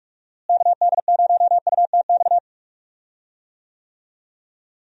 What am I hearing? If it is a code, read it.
Morse code, heard as a keyed tone, about 41 wpm, 700 Hz. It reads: KD0UTX
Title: USA Call Signs - 6 Characters - Repeated 2x - 45wpm